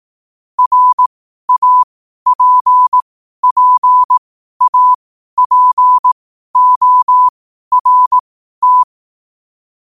Morse code, heard as a keyed tone, about 18 wpm, spaced as for 12 wpm, 1000 Hz. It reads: RAPPAPORT
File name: qotdfiles / 12wpm